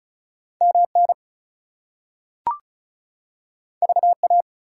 Morse code, heard as a keyed tone, about 36 wpm, 700 Hz. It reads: MN VA
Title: US States and Canadian Provinces 35wpm